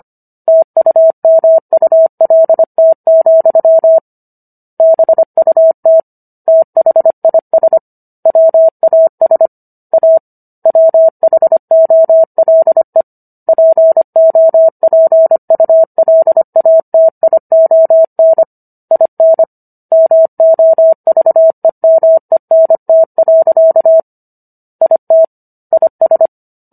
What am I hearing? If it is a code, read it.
Morse code, heard as a keyed tone, about 25 wpm, 650 Hz. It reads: TUMULT, BUT THIS WAS A WHOLE POPULATION IN MOVEMENT. IT IS